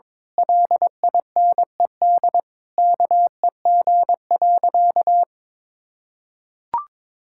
Morse code, heard as a keyed tone, about 22 wpm, 700 Hz. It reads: LINED KEG.
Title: Pangram Challenge v2 - 22wpm